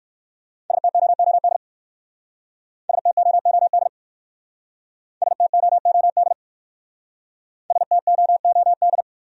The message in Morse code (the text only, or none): STOOD STOOD STOOD STOOD